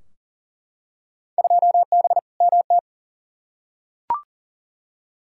Morse code, heard as a keyed tone, about 40 wpm, 700 Hz. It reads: 2B MT
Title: Field Day Exchanges Encore - 40wpm